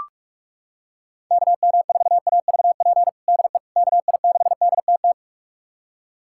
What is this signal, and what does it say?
Morse code, heard as a keyed tone, about 42 wpm, 700 Hz. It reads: KM4AVP DE KI6DTT
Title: USA Callsign DE Callsign - 6 Characters - 45wpm